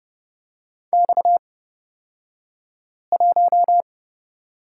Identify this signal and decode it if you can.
Morse code, heard as a keyed tone, about 30 wpm, spaced as for 5 wpm, 700 Hz. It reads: X1
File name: Lesson 37d - Letter x - Two Characters 30wpm